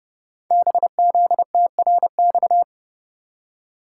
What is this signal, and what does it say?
Morse code, heard as a keyed tone, about 30 wpm, 700 Hz. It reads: BZTRX
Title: Code Groups - 5 Letter:Numbers - 30wpm